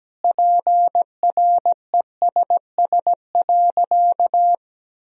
Morse code, heard as a keyed tone, about 17 wpm, 700 Hz. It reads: PRESS.
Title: Sentences from Top 1000 Words - Review Part 1 - 17wpm